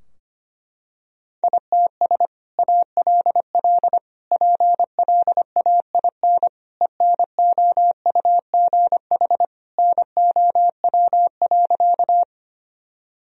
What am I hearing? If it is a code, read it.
Morse code, heard as a keyed tone, about 25 wpm, 700 Hz. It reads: ITS ALL PLAIN ENOUGH NOW.
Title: Sentences from Top 1000 Words - Review Part 2 - 25wpm